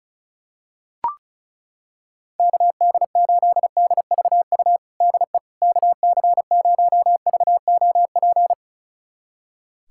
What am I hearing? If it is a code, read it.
Morse code, heard as a keyed tone, about 36 wpm, 700 Hz. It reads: KD8DVU DE KC0VOP
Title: USA Callsign DE Callsign 35wpm